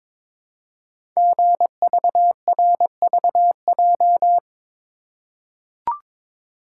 Morse code, heard as a keyed tone, about 22 wpm, 700 Hz. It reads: GVRVJ
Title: Code Groups - 5 Letter:Numbers - 22wpm